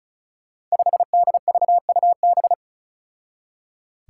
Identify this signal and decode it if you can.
Morse code, heard as a keyed tone, about 36 wpm, 700 Hz. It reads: 5DVUB
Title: Five Challenging Letter-Number Combinations 35wpm